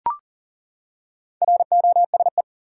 Morse code, heard as a keyed tone, about 39 wpm, 700 Hz. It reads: ROSE
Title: USA Common Names - Max 4 Letters - 40wpm